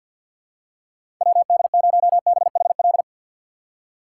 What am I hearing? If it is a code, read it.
Morse code, heard as a keyed tone, about 52 wpm, 700 Hz. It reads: WD0BHL
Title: USA Call Signs - 6 Characters - 6x Character Spacing - 50wpm